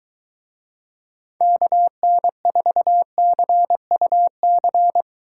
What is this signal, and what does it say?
Morse code, heard as a keyed tone, about 22 wpm, 700 Hz. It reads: KN4CUC